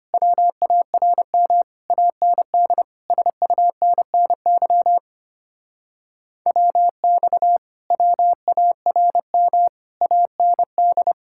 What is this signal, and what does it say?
Morse code, heard as a keyed tone, about 33 wpm, 700 Hz. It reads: WARM AND SUNNY WX WARM AND